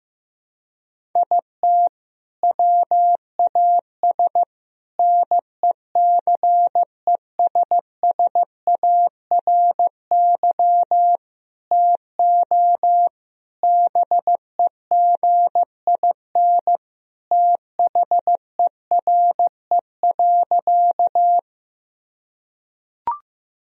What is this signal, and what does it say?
Morse code, heard as a keyed tone, about 15 wpm, 700 Hz. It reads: IT WAS NECESSARY TO BEGIN THERE.
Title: Sentences from Top 1000 Words - Review Part 1 - 15wpm